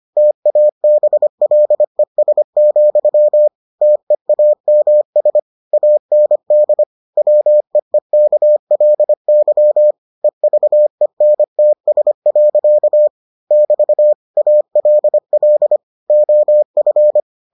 Morse code, heard as a keyed tone, about 25 wpm, 600 Hz. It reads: TABLES, TEAMS AND WEEKLY EVENTS. <BT> ALL OF